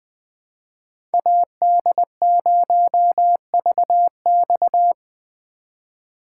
Morse code, heard as a keyed tone, about 20 wpm, 700 Hz. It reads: AD0VX
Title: USA Call Signs - 5 Characters - 4x Character Spacing - 20wpm